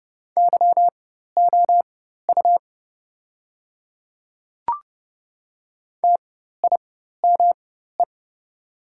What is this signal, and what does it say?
Morse code, heard as a keyed tone, about 30 wpm, spaced as for 14 wpm, 700 Hz. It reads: YOU TIME
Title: 4x Character Spacing - Top 300 Words - 30wpm